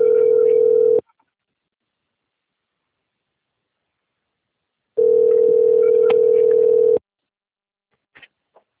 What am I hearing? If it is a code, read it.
Ringback tone (North America), 440 + 480 Hz.
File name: out- - - - - .28216